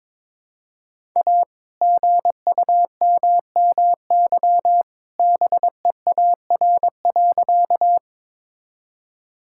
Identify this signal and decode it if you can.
Morse code, heard as a keyed tone, about 22 wpm, 700 Hz. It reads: A GUMMY BEAR.
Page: morse-code-ninja-practice-22wpm